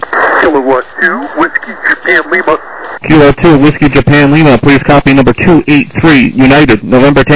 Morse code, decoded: none